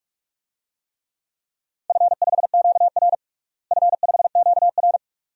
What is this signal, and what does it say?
Morse code, heard as a keyed tone, about 44 wpm, 700 Hz. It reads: F5QR F5QR
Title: DXCC-100 Call Signs - 4 Characters - Repeated 2x - 45wpm